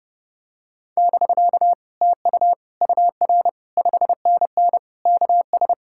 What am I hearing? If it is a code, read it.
Morse code, heard as a keyed tone, about 30 wpm, 700 Hz. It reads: <BK> TU UR 5NN KS